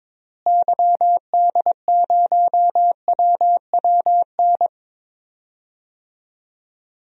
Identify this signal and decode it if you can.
Morse code, heard as a keyed tone, about 21 wpm, 700 Hz. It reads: YD0WWN
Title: DXCC-100 Call Signs - 6 Characters - Repeated 3x - 22wpm